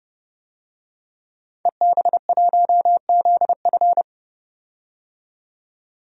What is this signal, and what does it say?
Morse code, heard as a keyed tone, about 29 wpm, 700 Hz. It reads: EB1ZF
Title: DXCC-100 Call Signs - 5 Characters - 30wpm